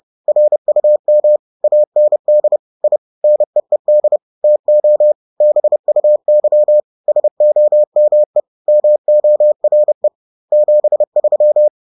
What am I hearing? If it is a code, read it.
Morse code, heard as a keyed tone, about 30 wpm, 600 Hz. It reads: RUM AND I NEED TO BUY SOME MORE 73